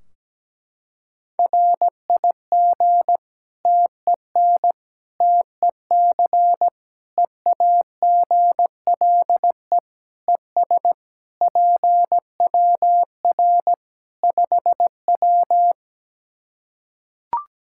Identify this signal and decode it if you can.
Morse code, heard as a keyed tone, about 17 wpm, 700 Hz. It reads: RIG TEN TEC EAGLE ES PWR 5W
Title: Practice QSO - Exchange 2 - Info - 17wpm